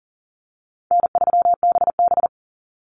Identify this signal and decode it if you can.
Morse code, heard as a keyed tone, about 42 wpm, 700 Hz. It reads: N3BB
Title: Easy Call Signs - 40wpm